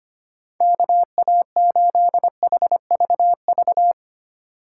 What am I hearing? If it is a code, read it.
Morse code, heard as a keyed tone, about 25 wpm, 700 Hz. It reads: KA8HVV
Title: USA Call Signs - 2x Character Spacing - 25wpm